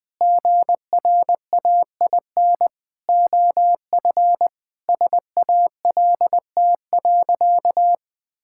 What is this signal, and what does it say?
Morse code, heard as a keyed tone, about 20 wpm, 700 Hz. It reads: GRAIN OF SALT.
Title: Sentences with 8-Word Phrases - Spoken Beginning - 20wpm